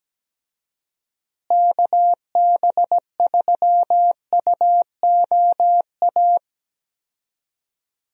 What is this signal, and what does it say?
Morse code, heard as a keyed tone, about 17 wpm, 700 Hz. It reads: KB3UOA
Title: USA Call Signs - 6 Characters - Repeated 2x - 17wpm